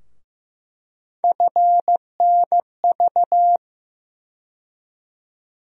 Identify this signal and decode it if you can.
Morse code, heard as a keyed tone, about 15 wpm, 700 Hz. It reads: FNV
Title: USA Call Sign Postfixes - 3 Letters 15wpm